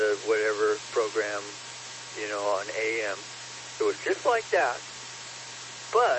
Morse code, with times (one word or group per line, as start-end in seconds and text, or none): none